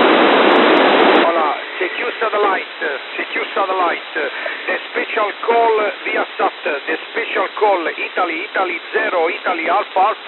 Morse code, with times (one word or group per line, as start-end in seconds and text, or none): none